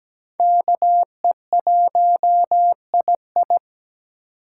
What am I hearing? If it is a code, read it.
Morse code, heard as a keyed tone, about 17 wpm, 700 Hz. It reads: KE1II
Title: USA Call Signs - 6x Character Spacing - 17wpm